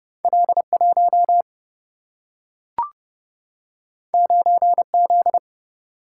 Morse code, heard as a keyed tone, about 30 wpm, 700 Hz. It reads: L1 9Z
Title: Single-Letter Single-Number 30wpm